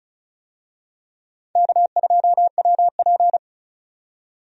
Morse code, heard as a keyed tone, about 36 wpm, 700 Hz. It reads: K2WP
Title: USA Call Signs - 4 Characters - 35wpm